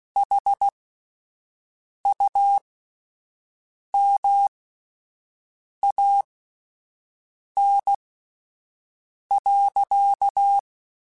Morse code, heard as a keyed tone, about 16 wpm, spaced as for 5 wpm, 800 Hz. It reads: HUMAN.
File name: Spaced / Chuckles Spaced-mp3s / HumanErr sp @10